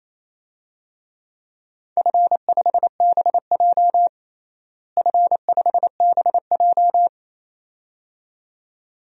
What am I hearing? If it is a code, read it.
Morse code, heard as a keyed tone, about 28 wpm, 700 Hz. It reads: F5BJ F5BJ
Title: DXCC-100 Call Signs - 4 Characters - Repeated 2x - 28wpm